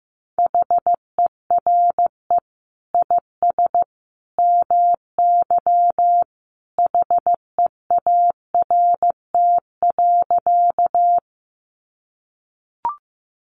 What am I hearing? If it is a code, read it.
Morse code, heard as a keyed tone, about 15 wpm, 700 Hz. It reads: HERE IS MY HEART.
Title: Sentences from Top 1200 Words 15wpm